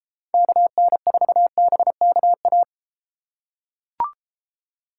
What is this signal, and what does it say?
Morse code, heard as a keyed tone, about 31 wpm, 700 Hz. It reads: KN4BKA